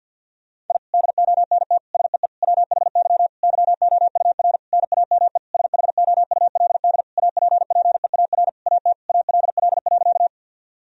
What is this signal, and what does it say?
Morse code, heard as a keyed tone, about 51 wpm, 700 Hz. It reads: I DONT SEE WHY YOUR NAME SHOULD APPEAR AT ALL.
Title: Sentences from Top 1000 Words - Review Part 2 - 50wpm